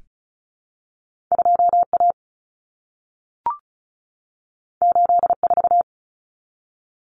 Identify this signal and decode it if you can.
Morse code, heard as a keyed tone, about 36 wpm, 700 Hz. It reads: 2A 84
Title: Common Cut Numbers - 2-Digit 35wpm